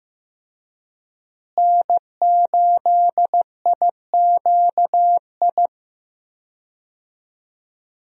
Morse code, heard as a keyed tone, about 15 wpm, 700 Hz. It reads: N8IQI
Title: USA Call Signs - 5 Characters - Repeated 2x - 15wpm